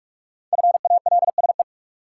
Morse code, heard as a keyed tone, about 44 wpm, 700 Hz. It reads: FALSE
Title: Sentences from Top 1600 Words - New Word Drill Alt - 45wpm